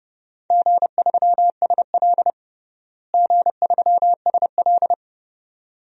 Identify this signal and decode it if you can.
Morse code, heard as a keyed tone, about 30 wpm, 700 Hz. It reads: G3SL G3SL
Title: DXCC-100 Call Signs - 4 Characters - Repeated 2x - 30wpm